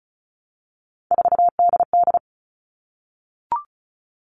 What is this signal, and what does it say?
Morse code, heard as a keyed tone, about 34 wpm, 700 Hz. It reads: 4DD
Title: All Cut Numbers - 3-Digit 35wpm